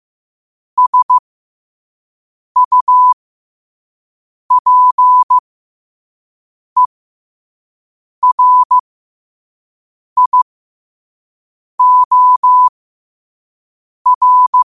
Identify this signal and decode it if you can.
Morse code, heard as a keyed tone, about 15 wpm, spaced as for 5 wpm, 1000 Hz. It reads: SUPERIOR
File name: qotdfiles / 5wpm